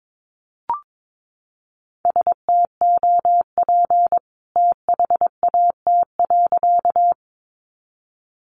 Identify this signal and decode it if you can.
Morse code, heard as a keyed tone, about 22 wpm, 700 Hz. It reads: STOP THAT.
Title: A Sound Of Thunder 22wpm